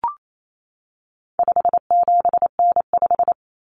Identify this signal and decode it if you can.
Morse code, heard as a keyed tone, about 28 wpm, 700 Hz. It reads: 57N5
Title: Common Cut Numbers - 4-Digit 28wpm